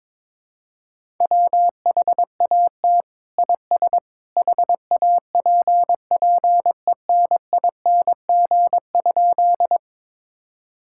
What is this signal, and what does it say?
Morse code, heard as a keyed tone, about 22 wpm, 700 Hz. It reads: WHAT IS HAPPENING?
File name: practice / 22wpm / Sentences with 2-Word Phrases - 22wpm